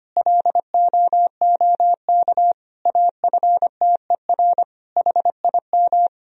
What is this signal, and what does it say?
Morse code, heard as a keyed tone, about 25 wpm, 700 Hz. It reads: LOOK AFTER HIM
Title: Sentences from Top 1000 Words - Review Part 1 - 25wpm